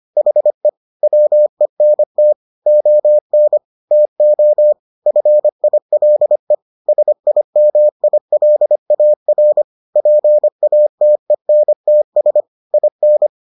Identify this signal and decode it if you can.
Morse code, heard as a keyed tone, about 25 wpm, 600 Hz. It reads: HE WENT ON TO FILE SIMILAR PATENTS IN